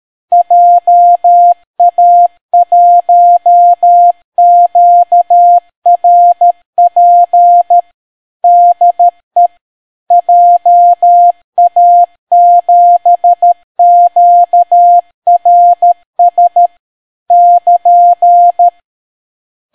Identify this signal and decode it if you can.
Morse code, heard as a keyed tone, about 13 wpm, 700 Hz. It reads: JA1QRP DE JA7QRS <KN>